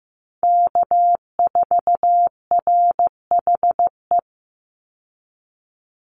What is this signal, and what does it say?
Morse code, heard as a keyed tone, about 15 wpm, 700 Hz. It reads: K4RHE
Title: USA Call Signs 15wpm